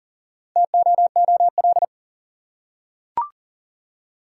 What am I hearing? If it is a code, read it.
Morse code, heard as a keyed tone, about 40 wpm, 700 Hz. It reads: TOOL
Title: Four Letter Words - 40wpm